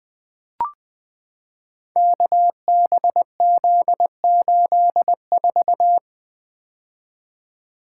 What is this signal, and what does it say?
Morse code, heard as a keyed tone, about 20 wpm, 700 Hz. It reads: KBZ84
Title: Code Groups - 5 Letter:Numbers - 20wpm